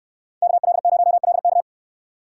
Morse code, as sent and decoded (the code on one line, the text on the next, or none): -.- .-.. ----- .-.. -.-
KL0LK